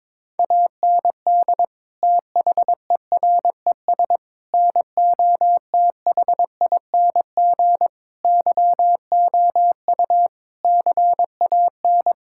Morse code, heard as a keyed tone, about 22 wpm, 700 Hz. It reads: AND THERES NOTHING YOU CAN
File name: practice / 22wpm / Sets of 8 Words - 22wpm